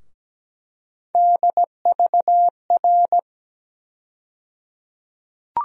Morse code, heard as a keyed tone, about 17 wpm, 700 Hz. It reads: DVR
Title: USA Call Sign Postfixes - 3 Letters 17wpm